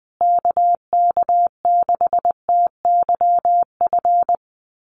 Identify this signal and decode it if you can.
Morse code, heard as a keyed tone, about 20 wpm, 700 Hz. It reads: KK6TYF
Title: USA Call Signs Repeated 2x 20wpm